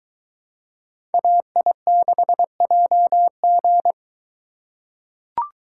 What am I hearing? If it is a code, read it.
Morse code, heard as a keyed tone, about 22 wpm, 700 Hz. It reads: AI6JG